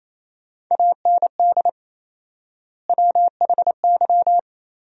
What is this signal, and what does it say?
Morse code, heard as a keyed tone, about 28 wpm, 700 Hz. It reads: AND WHY
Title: 4x Word-Spacing - Sentences from Top 400 Words - 28wpm